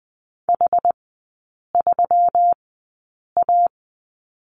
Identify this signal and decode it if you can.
Morse code, heard as a keyed tone, about 20 wpm, spaced as for 8 wpm, 700 Hz. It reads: H3A
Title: 191 Lesson 31e - Letter v - Three Characters 20wpm